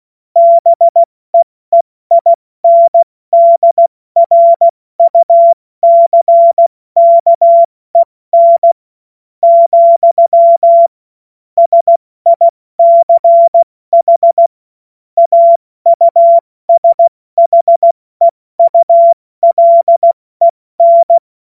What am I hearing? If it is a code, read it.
Morse code, heard as a keyed tone, about 16 wpm, spaced as for 14 wpm, 680 Hz. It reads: BEEINDRUCKEN, SICH AUSHEULEN
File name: Morsekurs / Witze / Witz 24.Oktober 2023 70 F80